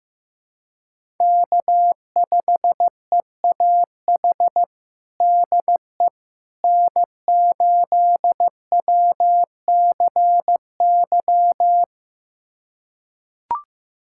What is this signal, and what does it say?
Morse code, heard as a keyed tone, about 15 wpm, 700 Hz. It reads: K5EAH DE N8WCY
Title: USA Callsign DE Callsign - 5 Characters - 15wpm